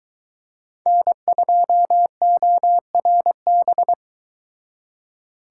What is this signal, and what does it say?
Morse code, heard as a keyed tone, about 22 wpm, 700 Hz. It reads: N2ORB